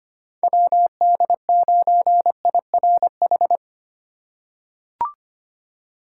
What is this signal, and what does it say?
Morse code, heard as a keyed tone, about 25 wpm, 700 Hz. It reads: WD9IRH
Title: USA Call Signs - 6 Characters - 2x Character Spacing - 25wpm